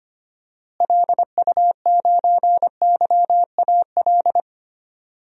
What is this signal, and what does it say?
Morse code, heard as a keyed tone, about 25 wpm, 700 Hz. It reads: LU9YAL
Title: DXCC-100 Call Signs - 6 Characters - Repeated 3x - 25wpm